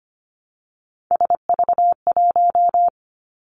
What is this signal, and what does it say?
Morse code, heard as a keyed tone, about 25 wpm, 700 Hz. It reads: SV1
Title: Callsign Prefixes for Top 50 DXCC - 25wpm